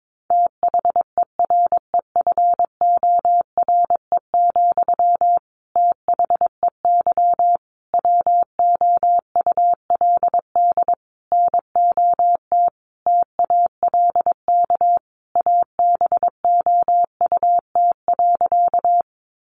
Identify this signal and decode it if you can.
Morse code, heard as a keyed tone, about 22 wpm, 700 Hz. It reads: THEREFORE, THEY WOULD NOT TALK ABOUT.
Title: Sentences from Top 800 Words 22wpm